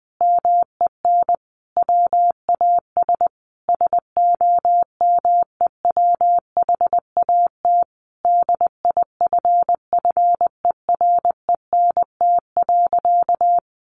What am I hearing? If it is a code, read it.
Morse code, heard as a keyed tone, about 20 wpm, 700 Hz. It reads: MEN WAS SOMEWHAT DIFFERENT.
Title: Sentences from Top 1300 Words 20wpm